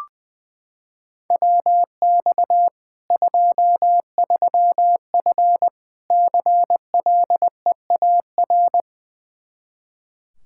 Morse code, heard as a keyed tone, about 20 wpm, 700 Hz. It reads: WX 23F CLEAR
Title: K1USN SST QSO Practice 20wpm